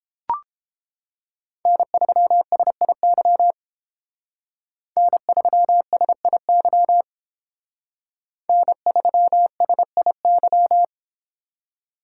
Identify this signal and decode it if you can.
Morse code, heard as a keyed tone, about 30 wpm, 700 Hz. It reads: N3SIY N3SIY N3SIY